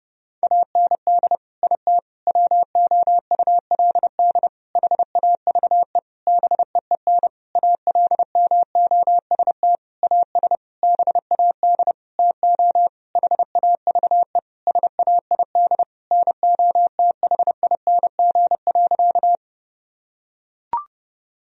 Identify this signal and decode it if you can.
Morse code, heard as a keyed tone, about 30 wpm, 700 Hz. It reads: AND IT WOULD HAVE BEEN ALMOST AS BAD TO HAVE SAID NOTHING.
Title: Sentences from Top 1000 Words - Review Part 4 - 30wpm